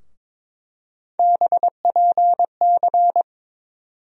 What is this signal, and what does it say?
Morse code, heard as a keyed tone, about 22 wpm, 700 Hz. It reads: BPC
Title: USA Call Sign Postfixes - 3 Letters 22wpm